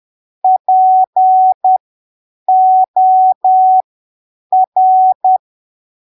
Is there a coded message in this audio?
Yes: Morse code from 0.4 to 5.4 s, heard as a keyed tone, about 10 wpm, spaced as for 7 wpm, 750 Hz. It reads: POR